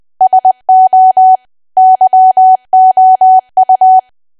Morse code, heard as a keyed tone, about 20 wpm, 750 Hz. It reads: SO YOU
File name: cwa / W1AW-Level3 / 20WPM